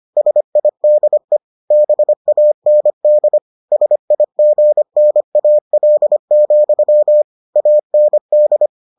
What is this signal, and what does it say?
Morse code, heard as a keyed tone, about 25 wpm, 600 Hz. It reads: SIDE BAND SIGNAL, AND